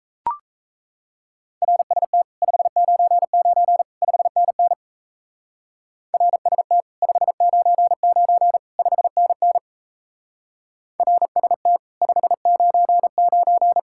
Morse code, heard as a keyed tone, about 34 wpm, 700 Hz. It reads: RST 599 5NN RST 599 5NN RST 599
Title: Speed-Racing Common QSO Phrases Encore - 28wpm